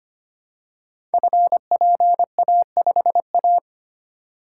Morse code, heard as a keyed tone, about 25 wpm, 700 Hz. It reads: FPA5A
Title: Code Groups - 5 Letter:Numbers - 25wpm